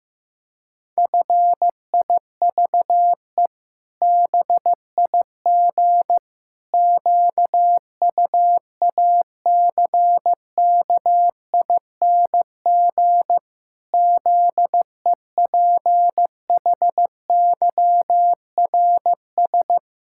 Morse code, heard as a keyed tone, about 14 wpm, 700 Hz. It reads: FIVE BIG QUACKING ZEPHYRS
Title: Pangram Challenge v2 - 15wpm